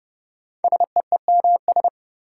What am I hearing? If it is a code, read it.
Morse code, heard as a keyed tone, about 30 wpm, 700 Hz. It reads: SEEMS
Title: Sets of 3 Words - New Word Drill - 30wpm